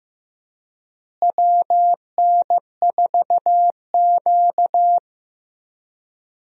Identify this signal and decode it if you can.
Morse code, heard as a keyed tone, about 15 wpm, 700 Hz. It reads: WN4Q
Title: USA Call Signs - 4 Characters - Repeated 3x - 15wpm